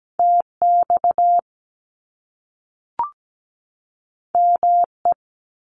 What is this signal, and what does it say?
Morse code, heard as a keyed tone, about 17 wpm, 700 Hz. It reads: TX ME
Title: US States 17wpm